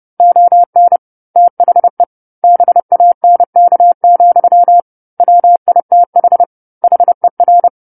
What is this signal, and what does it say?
Morse code, heard as a keyed tone, about 30 wpm, 700 Hz. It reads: ON THE BANK, WITH HER
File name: 10-Chapter-10 30 WPM